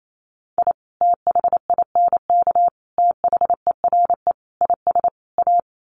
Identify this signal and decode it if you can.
Morse code, heard as a keyed tone, about 28 wpm, 700 Hz. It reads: I THINK THERE IS A
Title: Sets of 5 Words 28wpm